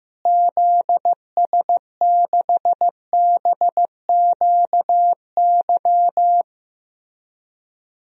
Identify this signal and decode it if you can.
Morse code, heard as a keyed tone, about 15 wpm, 700 Hz. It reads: ZS6BQY
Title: DXCC-100 Call Signs - 6 Characters - Repeated 3x - 15wpm